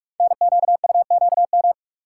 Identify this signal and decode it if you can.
Morse code, heard as a keyed tone, about 44 wpm, 700 Hz. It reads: NQUQM
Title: Code Groups - 5 Letters - 45wpm